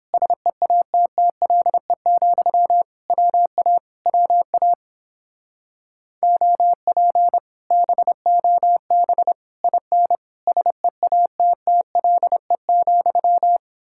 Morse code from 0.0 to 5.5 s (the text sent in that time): SEATTLE, WA WA